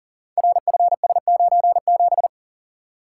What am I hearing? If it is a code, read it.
Morse code, heard as a keyed tone, about 38 wpm, 700 Hz. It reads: RFS97
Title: Code Groups - 5 Letter:Numbers - 40wpm